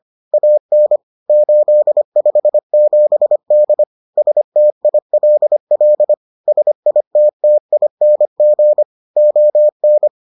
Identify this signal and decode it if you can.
Morse code, heard as a keyed tone, about 25 wpm, 600 Hz. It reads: AN 857D STILL SITTING ON